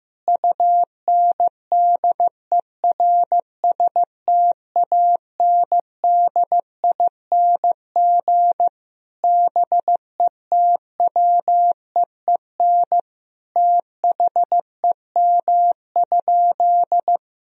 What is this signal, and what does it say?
Morse code, heard as a keyed tone, about 16 wpm, 700 Hz. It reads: UNDERSTANDING BETWEEN THEM?
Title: Sentences from Top 1000 Words - Review Part 3 - 15wpm